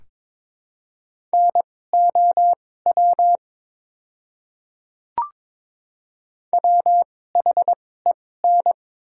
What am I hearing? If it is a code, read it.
Morse code, heard as a keyed tone, about 22 wpm, spaced as for 16 wpm, 700 Hz. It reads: NOW WHEN
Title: 2x Character Spacing - Top 300 Words - 22wpm